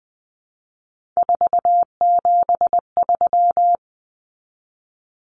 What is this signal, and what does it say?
Morse code, heard as a keyed tone, about 20 wpm, 700 Hz. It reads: 473
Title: Numbers - 3-Digit 20wpm